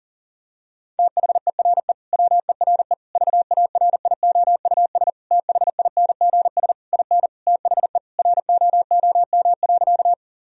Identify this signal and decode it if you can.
Morse code, heard as a keyed tone, about 40 wpm, 700 Hz. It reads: THERE WERE VARIOUS THINGS IN THE ROOM.
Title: Sentences from Top 1000 Words - Review Part 4 - 40wpm